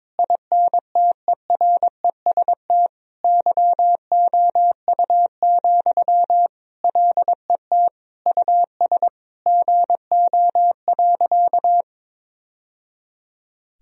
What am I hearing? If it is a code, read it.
Morse code, heard as a keyed tone, about 22 wpm, 700 Hz. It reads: INTEREST YOU, LET US GO.
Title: Sentences from Top 1000 Words - Review Part 2 - 22wpm